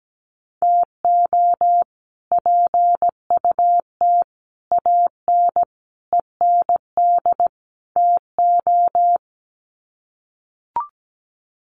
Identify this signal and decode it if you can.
Morse code, heard as a keyed tone, about 17 wpm, 700 Hz. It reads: TO PUT AN END TO
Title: Sets of 5 Words Encore2 17wpm